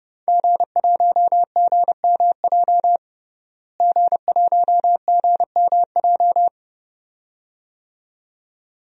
Morse code, heard as a keyed tone, about 32 wpm, 700 Hz. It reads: G1GMJ G1GMJ
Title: DXCC-100 Call Signs - 5 Characters - Repeated 2x - 30wpm